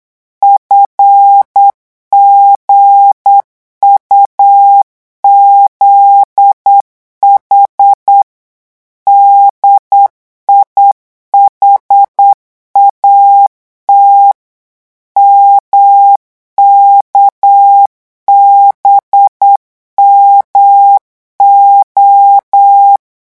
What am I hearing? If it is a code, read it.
Morse code, heard as a keyed tone, about 8 wpm, 800 Hz. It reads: FGUZH DIHAT MKBMO
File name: morse mp3 / revision8-vitesse 8 mots